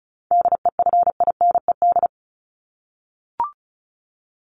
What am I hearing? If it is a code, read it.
Morse code, heard as a keyed tone, about 34 wpm, 700 Hz. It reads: DEFINED
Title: ED Suffix - Top 100 Words with Base Words 35wpm